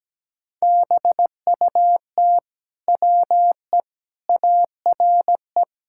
Morse code, heard as a keyed tone, about 17 wpm, 700 Hz. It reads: BUT WE ARE
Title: Sentences from Top 1600 Words 17wpm